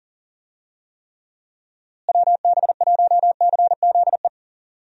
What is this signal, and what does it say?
Morse code, heard as a keyed tone, about 40 wpm, 700 Hz. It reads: WB1CZE
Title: USA Call Signs - 6 Characters - Repeated 3x - 40wpm